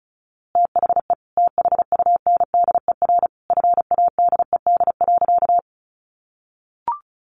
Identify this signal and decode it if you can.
Morse code, heard as a keyed tone, about 34 wpm, 700 Hz. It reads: THE THUNDER FADED.
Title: A Sound Of Thunder 35wpm